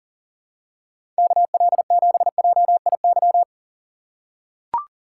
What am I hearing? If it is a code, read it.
Morse code, heard as a keyed tone, about 37 wpm, 700 Hz. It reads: KL7JIY
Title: USA Call Signs - 6 Characters - 2x Character Spacing - 40wpm